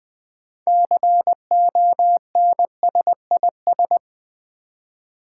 Morse code, heard as a keyed tone, about 20 wpm, 700 Hz. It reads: CONSIS
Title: Build A Word - 6 Letters At A Time - 12-Letter Words - 20wpm